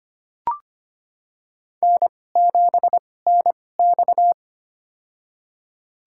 Morse code, heard as a keyed tone, about 25 wpm, spaced as for 18 wpm, 700 Hz. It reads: N7NX
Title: USA Call Signs - 4 Characters - 2x Character Spacing - 25wpm